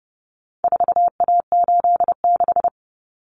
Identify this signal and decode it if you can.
Morse code, heard as a keyed tone, about 29 wpm, 700 Hz. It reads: 4A86
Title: Common Cut Numbers - 4-Digit 30wpm